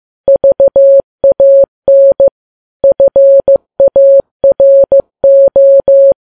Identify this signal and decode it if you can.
Morse code, heard as a keyed tone, about 15 wpm, 550 Hz. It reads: VAN FARO
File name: Files Web tls / Morse on9fv / morse15kps